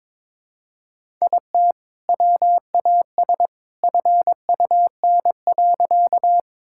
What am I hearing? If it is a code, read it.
Morse code, heard as a keyed tone, about 22 wpm, 700 Hz. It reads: IT WAS FUN.
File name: practice / 22wpm / Sentences with 2-Word Phrases - 22wpm